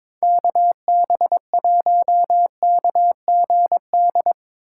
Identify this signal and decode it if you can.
Morse code, heard as a keyed tone, about 25 wpm, 700 Hz. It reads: KB1KGD
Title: Speed Racing - USA 2x3 Call Signs 22wpm